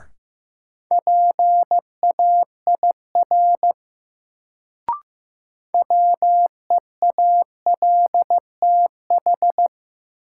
Morse code, heard as a keyed tone, about 15 wpm, 700 Hz. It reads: PAIR WEALTH
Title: Sentences from Top 1200 Words - New Word Drill Alt - 15wpm